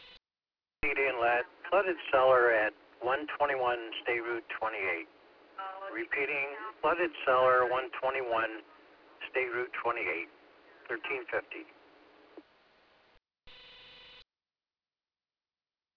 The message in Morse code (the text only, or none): none